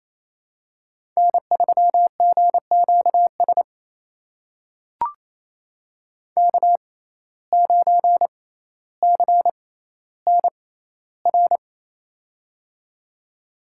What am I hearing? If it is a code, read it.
Morse code, heard as a keyed tone, about 27 wpm, spaced as for 14 wpm, 700 Hz. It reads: N3GQS K9CNR
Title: USA Call Signs - 5 Characters - 6x Character Spacing - 28wpm